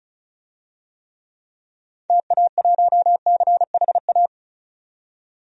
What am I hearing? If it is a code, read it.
Morse code, heard as a keyed tone, about 34 wpm, 700 Hz. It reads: TA1CHA
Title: DXCC-100 Call Signs - 6 Characters - 35wpm